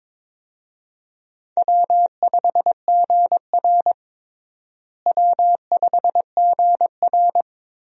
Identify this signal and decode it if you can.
Morse code, heard as a keyed tone, about 21 wpm, 700 Hz. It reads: W5GR W5GR
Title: USA Call Signs - 4 Characters - Repeated 3x - 22wpm